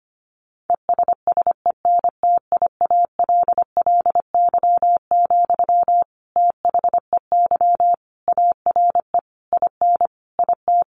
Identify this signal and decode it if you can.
Morse code, heard as a keyed tone, about 25 wpm, 700 Hz. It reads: ESSENTIALLY, THEY ARE IN IT
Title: Sentences from Top 1300 Words 25wpm